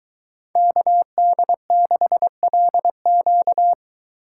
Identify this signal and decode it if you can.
Morse code, heard as a keyed tone, about 23 wpm, 700 Hz. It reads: KD6LQ